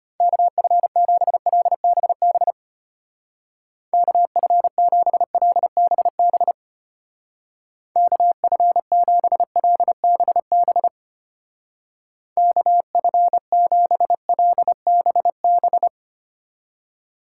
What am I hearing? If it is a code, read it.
Morse code, heard as a keyed tone, about 31 wpm, 700 Hz. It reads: KF7LBB KF7LBB KF7LBB KF7LBB
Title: Speed Racing - USA 2x3 Call Signs 25wpm